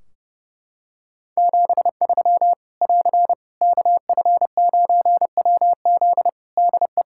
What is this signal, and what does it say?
Morse code, heard as a keyed tone, about 30 wpm, 700 Hz. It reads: 73 <AR> KF9WZ DE
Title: Practice QSO - Exchange 4 - Ending - 30wpm